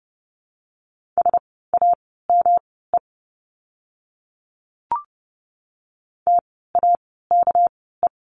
Morse code, heard as a keyed tone, about 30 wpm, spaced as for 17 wpm, 700 Hz. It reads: SAME TAKE
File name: course / 30:10wpm / 230 Lesson 37f - Letter x - Words 30wpm